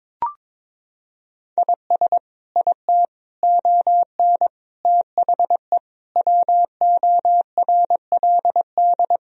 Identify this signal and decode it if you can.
Morse code, heard as a keyed tone, about 22 wpm, 700 Hz. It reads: IS IT ON THE WORLD
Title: Sentences from Top 1000 Words - Review Part 4 - 22wpm